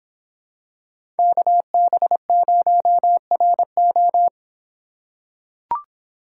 Morse code, heard as a keyed tone, about 25 wpm, 700 Hz. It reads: KB0RO